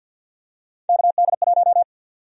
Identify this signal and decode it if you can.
Morse code, heard as a keyed tone, about 48 wpm, 700 Hz. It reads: KD1
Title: USA Call Sign Prefixes - 50wpm